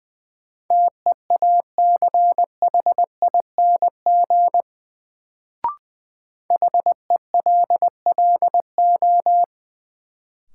Morse code, heard as a keyed tone, about 21 wpm, 700 Hz. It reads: TEACHING HELLO
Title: Sentences from Top 1100 Words - New Word Drill Alt - 20wpm